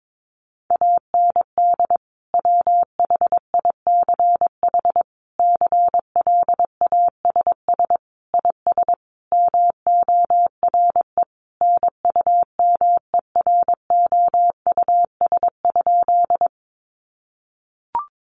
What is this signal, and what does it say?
Morse code, heard as a keyed tone, about 22 wpm, 700 Hz. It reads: AND WHICH CLASS IS MORE NUMEROUS?
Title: Sentences from Top 1400 Words 22wpm